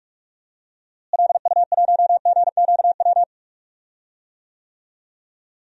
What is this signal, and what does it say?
Morse code, heard as a keyed tone, about 46 wpm, 700 Hz. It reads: LU1GQW
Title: DXCC-100 Call Signs - 6 Characters - Repeated 2x - 45wpm